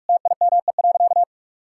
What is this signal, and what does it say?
Morse code, heard as a keyed tone, about 46 wpm, 700 Hz. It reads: TIME.
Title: Sentences from Top 1000 Words - Review Part 5 - 45wpm